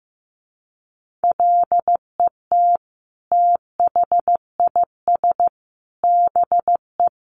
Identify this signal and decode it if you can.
Morse code, heard as a keyed tone, about 15 wpm, 700 Hz. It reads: LET THIS BE
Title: Sentences from Top 1200 Words 15wpm